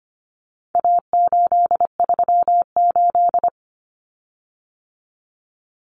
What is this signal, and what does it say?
Morse code, heard as a keyed tone, about 25 wpm, 700 Hz. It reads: A838
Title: Common Cut Numbers - 4-Digit 25wpm